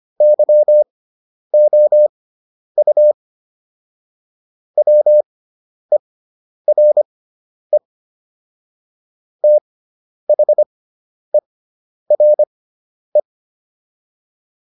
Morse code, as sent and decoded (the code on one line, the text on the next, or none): -.-- --- ..- / .-- . .-. / - .... .-. .
YOU WERE THERE